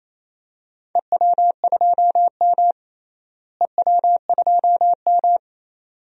Morse code, heard as a keyed tone, about 28 wpm, 700 Hz. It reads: EW2M EW2M
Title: DXCC-100 Call Signs - 4 Characters - Repeated 2x - 28wpm